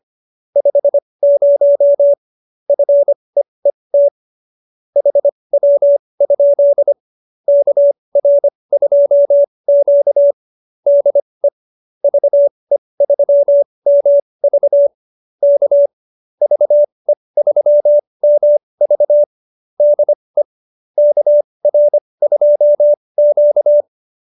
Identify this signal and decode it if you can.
Morse code, heard as a keyed tone, about 25 wpm, spaced as for 20 wpm, 570 Hz. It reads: 50 FEET HW? KR2Q DE VE3MV K VE3MV DE KR2Q
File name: qso220b 20